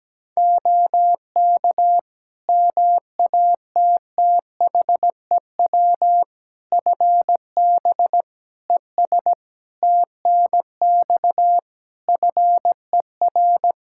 Morse code, heard as a keyed tone, about 17 wpm, 700 Hz. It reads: OK MATTHEW FB ES TNX FER
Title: Practice QSO - Exchange 2 - Info - 17wpm